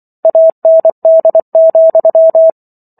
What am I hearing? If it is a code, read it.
Morse code, heard as a keyed tone, about 24 wpm, 650 Hz. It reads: AND,